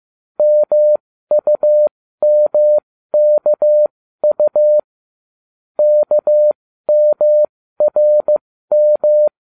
Morse code, heard as a keyed tone, about 15 wpm, spaced as for 13 wpm, 600 Hz. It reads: MUMKU KMRM